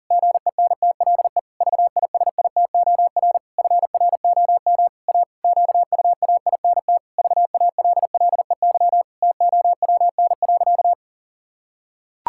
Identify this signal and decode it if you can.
Morse code, heard as a keyed tone, about 40 wpm, 700 Hz. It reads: GENTLE VISITOR FROM A QUAINT VALLEY TOWN.
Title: Pangram Challenge v2 - 40wpm